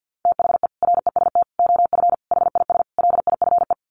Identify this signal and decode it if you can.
Morse code, heard as a keyed tone, about 48 wpm, 700 Hz. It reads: THE REST OF HIS LIFE
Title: Sets of 5 Words 50wpm